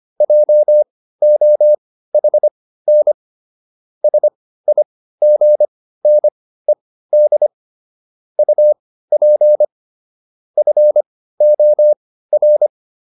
Morse code, heard as a keyed tone, about 25 wpm, spaced as for 15 wpm, 600 Hz. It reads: JOHN SIGNED UP FOR